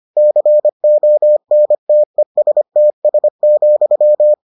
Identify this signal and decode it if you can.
Morse code, heard as a keyed tone, about 25 wpm, 600 Hz. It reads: CONTESTS,